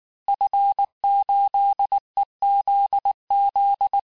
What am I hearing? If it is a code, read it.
Morse code, heard as a keyed tone, about 19 wpm, 800 Hz. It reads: F8EZZ
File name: cours-du-lundi files / cours-1595--FIN-de-QSO-vitesse-18mpm